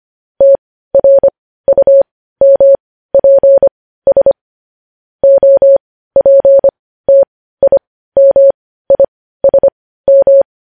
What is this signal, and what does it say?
Morse code, heard as a keyed tone, about 25 wpm, spaced as for 15 wpm, 550 Hz. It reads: TRUMPS OPTIMISM
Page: news headlines in morse code at 15 wpm